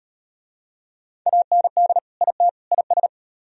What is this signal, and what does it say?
Morse code, heard as a keyed tone, about 33 wpm, 700 Hz. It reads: AND IT IS